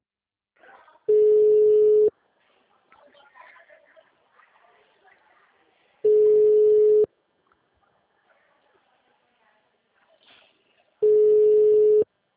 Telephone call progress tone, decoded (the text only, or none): ringback tone